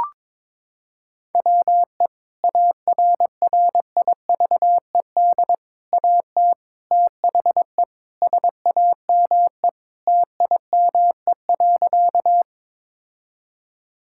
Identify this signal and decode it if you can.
Morse code, heard as a keyed tone, about 22 wpm, 700 Hz. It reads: WE ARRIVED AT THE SAME TIME.
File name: practice / 22wpm / Sentences with 3-Word Phrases - 22wpm